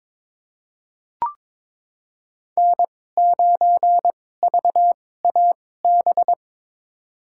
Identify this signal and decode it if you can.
Morse code, heard as a keyed tone, about 22 wpm, spaced as for 16 wpm, 700 Hz. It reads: N9VAB
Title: USA Call Signs - 5 Characters - 2x Character Spacing - 22wpm